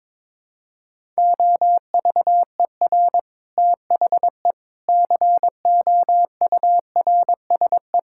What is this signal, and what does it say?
Morse code, heard as a keyed tone, about 22 wpm, 700 Hz. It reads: OVER THE COURSE
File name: practice / 22wpm / Sets of 6 Words - 22wpm